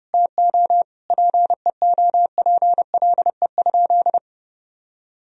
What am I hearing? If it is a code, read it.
Morse code, heard as a keyed tone, about 30 wpm, 700 Hz. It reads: TO PEOPLE?
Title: Sentences from Top 1000 Words - Review Part 2 - 30wpm